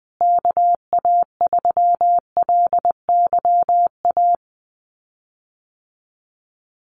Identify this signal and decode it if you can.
Morse code, heard as a keyed tone, about 20 wpm, 700 Hz. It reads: KA3LYA
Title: USA Call Signs Repeated 3x 20wpm